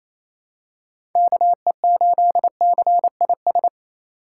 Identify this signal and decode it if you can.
Morse code, heard as a keyed tone, about 28 wpm, 700 Hz. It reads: KE8CIS
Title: USA Call Signs - 6 Characters - 2x Character Spacing - 28wpm